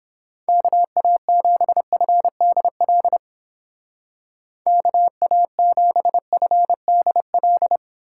Morse code, heard as a keyed tone, about 28 wpm, 700 Hz. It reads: KA7FDL KA7FDL
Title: Speed Racing - USA Call Signs 22wpm